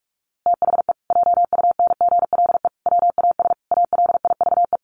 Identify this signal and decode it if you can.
Morse code, heard as a keyed tone, about 44 wpm, 700 Hz. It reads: THE JUNGLE WAS ALIVE
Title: A Sound Of Thunder 45wpm